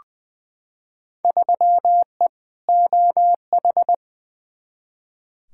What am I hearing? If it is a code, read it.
Morse code, heard as a keyed tone, about 20 wpm, 700 Hz. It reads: 3E OH
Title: Field Day Exchanges - 20wpm